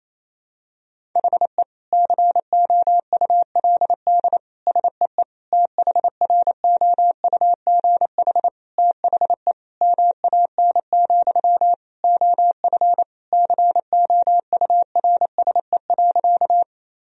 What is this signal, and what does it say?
Morse code, heard as a keyed tone, about 28 wpm, 700 Hz. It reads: HE COULD SEE THROUGH THE MAN, OF COURSE.
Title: Sentences from Top 1000 Words - Review Part 1 - 28wpm